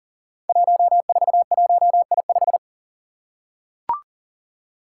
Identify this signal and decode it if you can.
Morse code, heard as a keyed tone, about 38 wpm, 700 Hz. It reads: 141I5
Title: Five Challenging Letter-Number Combinations 40wpm